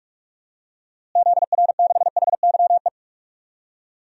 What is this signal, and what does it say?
Morse code, heard as a keyed tone, about 46 wpm, 700 Hz. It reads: ZR6HYE
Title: DXCC-100 Call Signs - 6 Characters - Repeated 2x - 45wpm